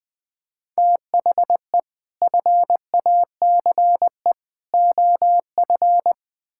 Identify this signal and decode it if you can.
Morse code, heard as a keyed tone, about 20 wpm, 700 Hz. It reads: THE FACE OF
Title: Build A Phrase - 4-Word Phrases - 20wpm